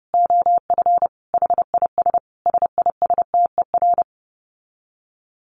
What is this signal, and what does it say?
Morse code, heard as a keyed tone, about 30 wpm, 700 Hz. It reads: OF HIS SISTER
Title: Dit Challenge 30wpm